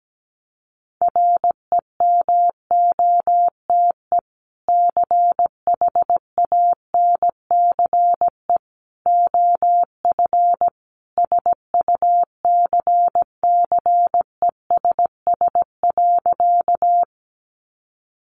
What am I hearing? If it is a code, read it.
Morse code, heard as a keyed tone, about 17 wpm, 700 Hz. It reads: REMOTE CHANCE OF SUCCESS.
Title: Sentences from Top 1300 Words 17wpm